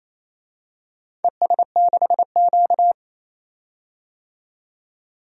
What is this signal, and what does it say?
Morse code, heard as a keyed tone, about 28 wpm, 700 Hz. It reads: ES6Q
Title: DXCC-100 Call Signs - 4 Characters - 28wpm